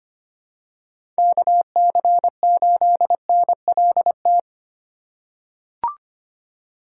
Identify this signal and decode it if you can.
Morse code, heard as a keyed tone, about 25 wpm, 700 Hz. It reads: KC8NLT
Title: USA Call Signs - 6 Characters - 4x Character Spacing - 25wpm